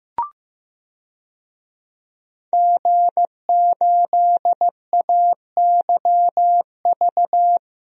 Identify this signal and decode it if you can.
Morse code, heard as a keyed tone, about 15 wpm, 700 Hz. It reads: G8AYV
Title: DXCC-100 Call Signs - 5 Characters - 15wpm